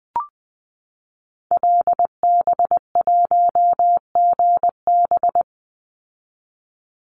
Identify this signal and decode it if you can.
Morse code, heard as a keyed tone, about 19 wpm, 700 Hz. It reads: LB1GB
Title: Call Signs Repeated 3x - 20wpm